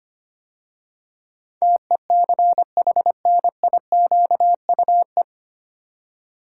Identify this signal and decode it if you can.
Morse code, heard as a keyed tone, about 25 wpm, 700 Hz. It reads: TECHNIQUE
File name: practice / 25wpm / 9 Letter Words - First 3 Letters Given - 25wpm